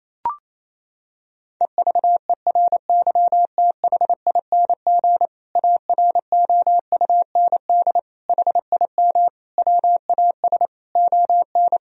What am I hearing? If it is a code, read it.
Morse code, heard as a keyed tone, about 28 wpm, 700 Hz. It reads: EVERYTHING AROUND HIM WAS ON
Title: Sentences from Top 1000 Words - Review Part 3 - 28wpm